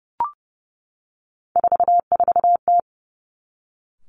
Morse code, heard as a keyed tone, about 30 wpm, 700 Hz. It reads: 44T
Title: Three Challenging Letter-Number Combinations 30wpm